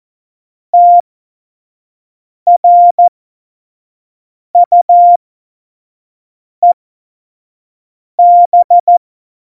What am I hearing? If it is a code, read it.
Morse code, heard as a keyed tone, about 14 wpm, spaced as for 5 wpm, 700 Hz. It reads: TRUEB
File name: Morsekurs / Witze / Witz 24.Dezember 2019, 25 F70